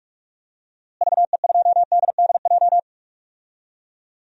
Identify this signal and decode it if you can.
Morse code, heard as a keyed tone, about 46 wpm, 700 Hz. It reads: VE2DDJ
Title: DXCC-100 Call Signs - 6 Characters - Repeated 3x - 45wpm